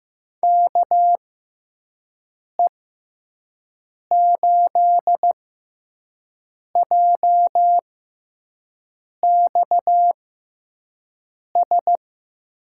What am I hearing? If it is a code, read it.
Morse code, heard as a keyed tone, about 15 wpm, spaced as for 5 wpm, 700 Hz. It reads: KE8JXS
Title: USA Call Signs - 6 Characters - 6x Character Spacing - 15wpm